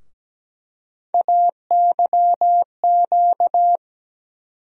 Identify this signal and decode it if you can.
Morse code, heard as a keyed tone, about 17 wpm, 700 Hz. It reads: AYQ
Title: USA Call Sign Postfixes - 3 Letters 17wpm